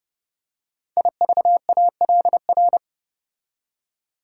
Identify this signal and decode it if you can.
Morse code, heard as a keyed tone, about 30 wpm, 700 Hz. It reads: IVALR
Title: Code Groups - 5 Letters - 30wpm